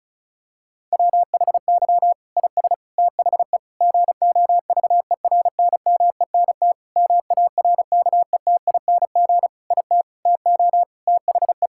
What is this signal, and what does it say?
Morse code, heard as a keyed tone, about 36 wpm, 700 Hz. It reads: WHY IS THE GOVERNMENT MARKETING IT TO THE
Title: Sentences from Top 1000 Words - Review Part 5 - 35wpm